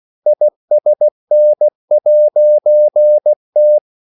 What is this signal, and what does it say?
Morse code, heard as a keyed tone, about 16 wpm, 600 Hz. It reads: ISN'T
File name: CW / Practice / EnglishQuotations0416